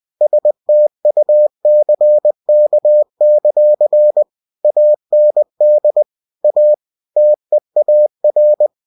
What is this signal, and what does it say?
Morse code, heard as a keyed tone, about 20 wpm, 600 Hz. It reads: STUCK; AND A TEAR
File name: cw-books / winnie-the-pooh / winnie-20wpm-page-0019